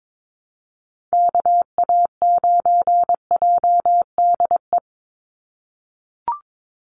Morse code, heard as a keyed tone, about 22 wpm, 700 Hz. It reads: KA9JDE
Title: USA Call Signs 22wpm